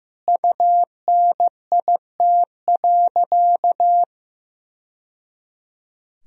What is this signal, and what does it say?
Morse code, heard as a keyed tone, about 15 wpm, 700 Hz. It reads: UNIT.
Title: Sentences from Top 1000 Words - Review Part 2 - 15wpm